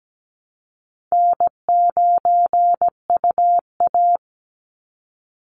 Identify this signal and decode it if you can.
Morse code, heard as a keyed tone, about 17 wpm, 700 Hz. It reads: N9UA
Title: Easy Call Signs Repeated 2x - 17wpm